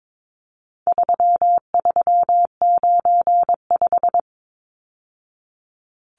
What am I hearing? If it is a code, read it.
Morse code, heard as a keyed tone, about 22 wpm, 700 Hz. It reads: 3395
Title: Numbers - 4-Digit 22wpm